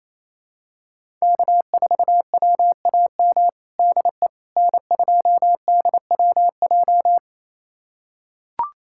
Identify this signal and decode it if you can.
Morse code, heard as a keyed tone, about 28 wpm, 700 Hz. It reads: K4WAM DE N2DWJ
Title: USA Callsign DE Callsign - 5 Characters - 28wpm